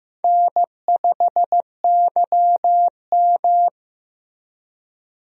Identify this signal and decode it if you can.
Morse code, heard as a keyed tone, about 15 wpm, 700 Hz. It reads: N5YM
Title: USA Call Signs - 4 Characters - Repeated 3x - 15wpm